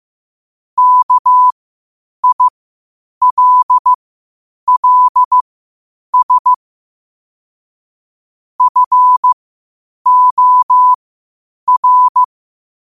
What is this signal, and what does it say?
Morse code, heard as a keyed tone, about 15 wpm, spaced as for 8 wpm, 1000 Hz. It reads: KILLS FOR
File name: qotdfiles / 8wpm